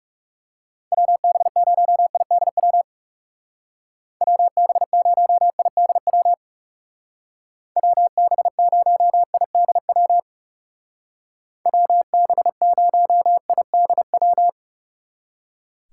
Morse code, heard as a keyed tone, about 35 wpm, 700 Hz. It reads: WB0IDW WB0IDW WB0IDW WB0IDW
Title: Speed Racing - USA 2x3 Call Signs 30wpm